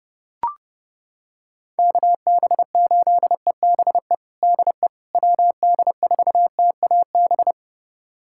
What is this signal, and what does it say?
Morse code, heard as a keyed tone, about 28 wpm, 700 Hz. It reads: KB8EBE DE WD4TAB
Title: USA Callsign DE Callsign - 6 Characters - 30wpm